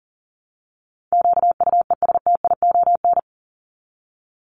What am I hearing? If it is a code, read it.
Morse code, heard as a keyed tone, about 40 wpm, 700 Hz. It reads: QUESTION
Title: ES:S Suffix - Top 100 Words with Base Words 40wpm